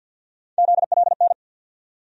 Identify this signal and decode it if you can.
Morse code, heard as a keyed tone, about 47 wpm, 700 Hz. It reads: ZLN
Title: USA Call Sign Postfixes - 3 Letters 50wpm